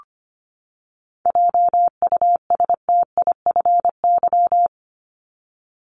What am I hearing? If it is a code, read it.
Morse code, heard as a keyed tone, about 25 wpm, 700 Hz. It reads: JUSTIFY
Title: Top 1751-2000 Words 25wpm